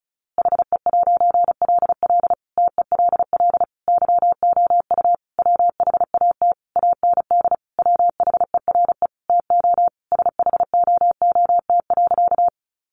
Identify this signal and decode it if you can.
Morse code, heard as a keyed tone, about 34 wpm, 700 Hz. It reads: HE'LL TELL YOU WHAT AND WHERE TO SHOOT.
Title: A Sound Of Thunder 35wpm